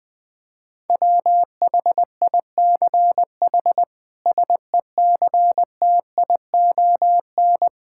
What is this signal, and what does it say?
Morse code, heard as a keyed tone, about 20 wpm, 700 Hz. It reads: WHICH SECTION
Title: Sentences from Top 1000 Words - Review Part 5 - 20wpm